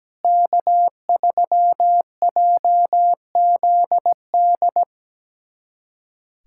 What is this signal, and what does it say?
Morse code, heard as a keyed tone, about 17 wpm, 700 Hz. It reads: K3JZD
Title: USA Callsign DE Callsign - 5 Characters - 17wpm